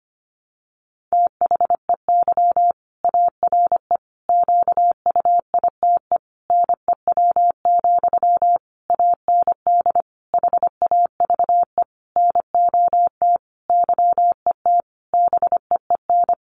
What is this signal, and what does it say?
Morse code, heard as a keyed tone, about 25 wpm, 700 Hz. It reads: THEY ARE QUITE NEW, AND HAVE NOT YET BEEN
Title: Sentences from Top 700 Words 25wpm